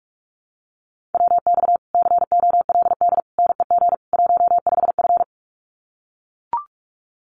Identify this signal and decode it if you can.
Morse code, heard as a keyed tone, about 44 wpm, 700 Hz. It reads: WX COLD NEG 15F
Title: Common QSO Phrases Encore 45wpm